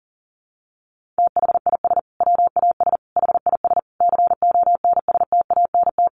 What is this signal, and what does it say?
Morse code, heard as a keyed tone, about 40 wpm, 700 Hz. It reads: THIS WAS HIS CONSTANT
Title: Sentences from Top 1500 Words 40wpm